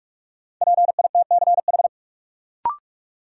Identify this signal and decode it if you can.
Morse code, heard as a keyed tone, about 46 wpm, 700 Hz. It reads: PITCH
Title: Sentences from Top 1300 Words - New Word Drill - 45wpm